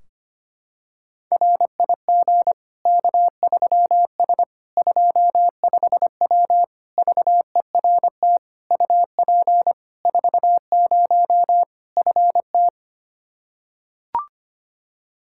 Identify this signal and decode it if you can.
Morse code, heard as a keyed tone, about 25 wpm, 700 Hz. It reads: RIG K3S 25W VERT UP 40 FT
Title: K1USN SST QSO Practice 25wpm